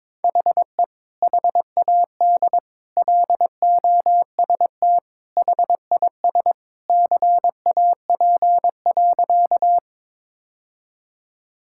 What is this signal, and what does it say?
Morse code, heard as a keyed tone, about 22 wpm, 700 Hz. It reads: HE HAD LOST HIS CAP.
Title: Sentences from Top 1600 Words 22wpm